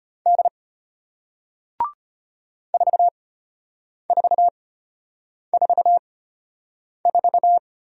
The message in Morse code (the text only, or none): D 4444